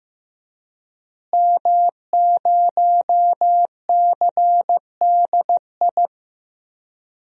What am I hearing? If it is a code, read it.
Morse code, heard as a keyed tone, about 15 wpm, 700 Hz. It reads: M0CDI
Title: DXCC-100 Call Signs - 5 Characters - Repeated 3x - 15wpm